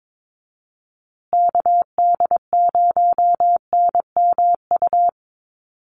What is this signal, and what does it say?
Morse code, heard as a keyed tone, about 22 wpm, 700 Hz. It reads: KD0NMU
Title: USA Call Signs Repeated 2x 22wpm